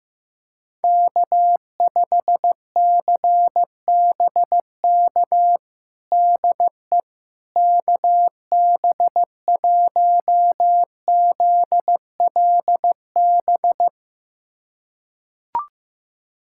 Morse code, heard as a keyed tone, about 15 wpm, 700 Hz. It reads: K5CBK DE KB1ZLB
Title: USA Callsign DE Callsign 15wpm